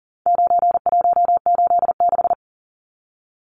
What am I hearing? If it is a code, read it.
Morse code, heard as a keyed tone, about 40 wpm, 700 Hz. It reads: 9186
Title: Numbers - 4-Digit 40wpm